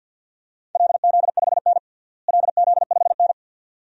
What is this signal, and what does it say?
Morse code, heard as a keyed tone, about 51 wpm, 700 Hz. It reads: LZ5N LZ5N
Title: DXCC-100 Call Signs - 4 Characters - Repeated 2x - 50wpm